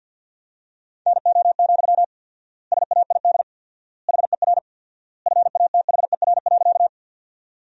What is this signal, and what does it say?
Morse code, heard as a keyed tone, about 47 wpm, 700 Hz. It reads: NO, SAID HER FATHER.
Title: 4x Word-Spacing - Sentences from Top 300 Words - 50wpm